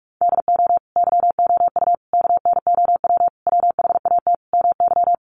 Morse code, heard as a keyed tone, about 44 wpm, 700 Hz. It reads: DO YOU KNOW WHAT MY
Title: Sentences from Top 1200 Words 45wpm